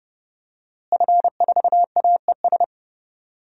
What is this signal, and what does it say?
Morse code, heard as a keyed tone, about 30 wpm, 700 Hz. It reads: F4AES
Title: DXCC-100 Call Signs - 5 Characters - Repeated 2x - 30wpm